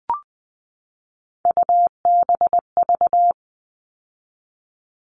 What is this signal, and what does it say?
Morse code, heard as a keyed tone, about 20 wpm, 700 Hz. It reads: UBV
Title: All Cut Numbers - 3-Digit 20wpm